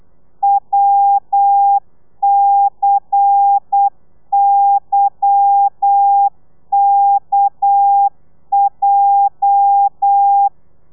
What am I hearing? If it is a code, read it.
Morse code, heard as a keyed tone, about 8 wpm, 800 Hz. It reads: WCYKJ
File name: telegrafia / i2tfj / Lez10 35